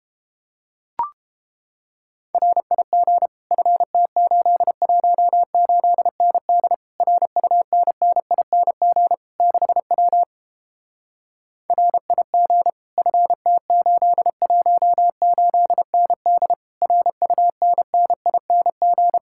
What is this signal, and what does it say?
Morse code, heard as a keyed tone, about 31 wpm, 700 Hz. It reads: RIG FT818ND RUNNING 6W RIG FT818ND RUNNING